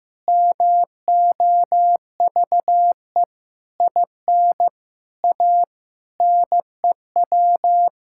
Morse code, heard as a keyed tone, about 15 wpm, 700 Hz. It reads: MOVE IN A NEW
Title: Sentences from Top 1000 Words - Review Part 5 - 15wpm